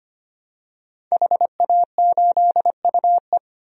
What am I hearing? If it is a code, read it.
Morse code, heard as a keyed tone, about 25 wpm, 700 Hz. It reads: HA8UE
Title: DXCC-100 Call Signs - 5 Characters - Repeated 2x - 25wpm